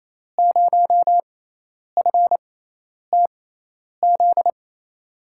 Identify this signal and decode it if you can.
Morse code, heard as a keyed tone, about 28 wpm, spaced as for 10 wpm, 700 Hz. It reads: 0FTZ
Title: USA Call Signs - 6 Characters - 6x Character Spacing - 28wpm